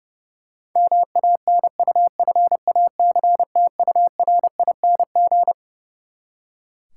Morse code, heard as a keyed tone, about 30 wpm, 700 Hz. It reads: MANUFACTURING
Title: Practice QSO - Exchange 3 - More Info - 30wpm